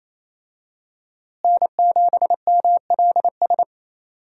Morse code, heard as a keyed tone, about 28 wpm, 700 Hz. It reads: N7MLS
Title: USA Call Signs - 5 Characters - Repeated 2x - 28wpm